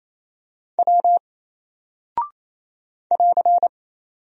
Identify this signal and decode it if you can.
Morse code, heard as a keyed tone, about 28 wpm, 700 Hz. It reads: W <AR>
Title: Single Character (Letter:Number:Punctuation:Prosign) Alt 28wpm